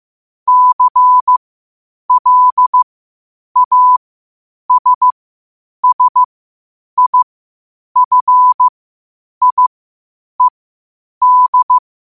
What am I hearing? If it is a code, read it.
Morse code, heard as a keyed tone, about 15 wpm, spaced as for 8 wpm, 1000 Hz. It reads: CLASSIFIED
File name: qotdfiles / 8wpm